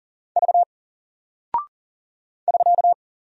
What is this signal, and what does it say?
Morse code, heard as a keyed tone, about 40 wpm, 700 Hz. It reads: V <SK>
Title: Single Character (Letter:Number:Punctuation:Prosign) Alt 40wpm